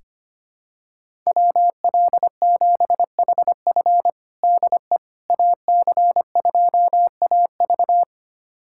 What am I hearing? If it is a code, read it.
Morse code, heard as a keyed tone, about 25 wpm, 700 Hz. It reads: WL7HF DE AC2AV
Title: USA Callsign DE Callsign - 5 Characters - 25wpm